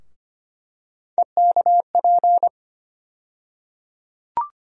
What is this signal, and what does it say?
Morse code, heard as a keyed tone, about 25 wpm, 700 Hz. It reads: EKP
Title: USA Call Sign Postfixes - 3 Letters 25wpm